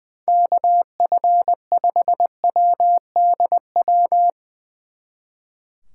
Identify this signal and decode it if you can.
Morse code, heard as a keyed tone, about 21 wpm, 700 Hz. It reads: KF5WDW
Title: Speed Racing - USA 2x3 Call Signs 20wpm